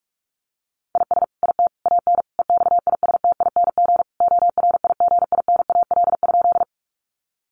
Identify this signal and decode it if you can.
Morse code, heard as a keyed tone, about 44 wpm, 700 Hz. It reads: IS IT AN EXISTING ORIGINAL?
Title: Sentences from Top 1500 Words 45wpm